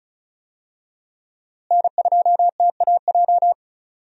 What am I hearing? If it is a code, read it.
Morse code, heard as a keyed tone, about 36 wpm, 700 Hz. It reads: N2TAJ
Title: USA Call Signs - 5 Characters - Repeated 2x - 35wpm